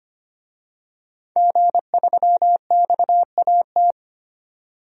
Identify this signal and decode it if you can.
Morse code, heard as a keyed tone, about 25 wpm, 700 Hz. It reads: G3XAT
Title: DXCC-100 Call Signs - 5 Characters - 25wpm